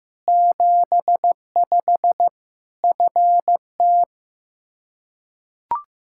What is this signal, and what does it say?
Morse code, heard as a keyed tone, about 15 wpm, 700 Hz. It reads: 75 FT
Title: Practice QSO - Exchange 2 - Info - 15wpm